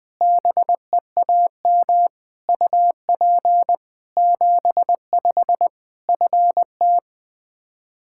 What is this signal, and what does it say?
Morse code, heard as a keyed tone, about 20 wpm, 700 Hz. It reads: BEAM UP 75 FT
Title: Practice QSO - Exchange 2 - Info - 20wpm